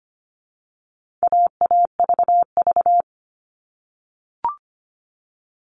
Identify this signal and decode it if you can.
Morse code, heard as a keyed tone, about 25 wpm, 700 Hz. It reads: AAVV
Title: All Cut Numbers - 4-Digit 25wpm